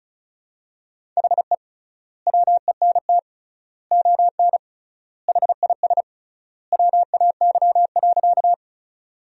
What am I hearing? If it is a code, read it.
Morse code, heard as a keyed tone, about 36 wpm, 700 Hz. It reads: HE WENT ON HIS WAY.
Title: 3x Word-Spacing - Sentences from Top 200 Words - 35wpm